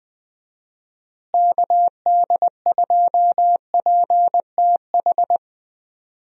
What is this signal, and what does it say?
Morse code, heard as a keyed tone, about 20 wpm, 700 Hz. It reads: KD2PTH
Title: USA Call Signs - 6 Characters - Repeated 3x - 20wpm